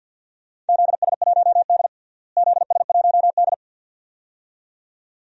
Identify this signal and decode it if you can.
Morse code, heard as a keyed tone, about 52 wpm, 700 Hz. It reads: ZS1D ZS1D
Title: DXCC-100 Call Signs - 4 Characters - Repeated 2x - 50wpm